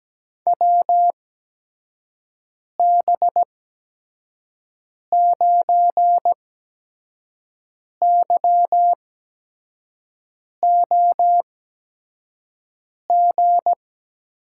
Morse code, heard as a keyed tone, about 17 wpm, spaced as for 5 wpm, 700 Hz. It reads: WB9YOG
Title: USA Call Signs - 6 Characters - 8x Character Spacing - 17wpm